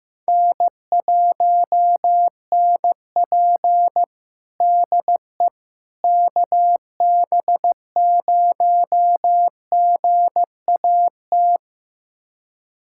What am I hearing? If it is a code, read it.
Morse code, heard as a keyed tone, about 15 wpm, 700 Hz. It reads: N1NP DE KB0GAT
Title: USA Callsign DE Callsign 15wpm